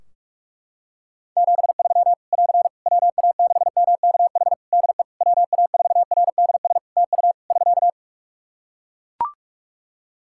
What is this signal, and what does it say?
Morse code, heard as a keyed tone, about 43 wpm, 700 Hz. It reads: 73 <AR> WA6MKH DE WA4RDS TU <SK>
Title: Practice QSO - Exchange 4 - Ending - 45wpm